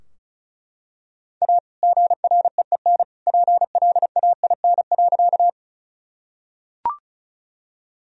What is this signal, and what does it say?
Morse code, heard as a keyed tone, about 36 wpm, 700 Hz. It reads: A GREEN PLAIN.
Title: Sentences from Top 1000 Words - Review Part 2 - 35wpm